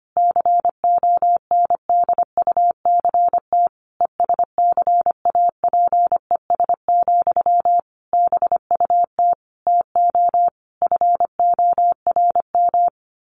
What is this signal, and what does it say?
Morse code, heard as a keyed tone, about 25 wpm, 700 Hz. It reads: CONDUCT ESCAPES, BUT TO FORM
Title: A Sound Of Thunder 25wpm